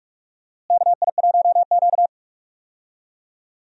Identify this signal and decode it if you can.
Morse code, heard as a keyed tone, about 44 wpm, 700 Hz. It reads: KI1Q
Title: USA Call Signs - 4 Characters - Repeated 2x - 45wpm